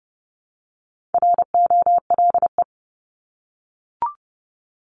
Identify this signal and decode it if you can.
Morse code, heard as a keyed tone, about 30 wpm, 700 Hz. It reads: ROLE
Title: Top 751-1000 Words 30wpm